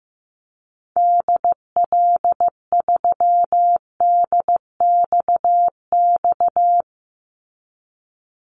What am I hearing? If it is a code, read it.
Morse code, heard as a keyed tone, about 15 wpm, 700 Hz. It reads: DL3DXX
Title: Call Signs Repeated 3x - 15wpm